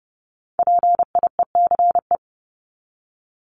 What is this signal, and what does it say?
Morse code, heard as a keyed tone, about 30 wpm, 700 Hz. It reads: PIECE
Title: Top 500 Words 30wpm